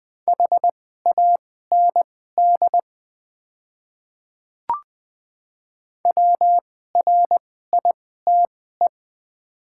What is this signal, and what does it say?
Morse code, heard as a keyed tone, about 20 wpm, spaced as for 14 wpm, 690 Hz. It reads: HAND WRITE
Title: 2x Character Spacing - Top 500 Words - 20wpm